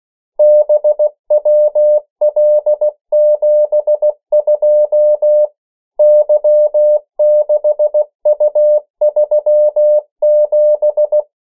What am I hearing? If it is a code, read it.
Morse code, heard as a keyed tone, about 16 wpm, 600 Hz. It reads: BWL72 Y6U37
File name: morse mp3 / 14 urok 13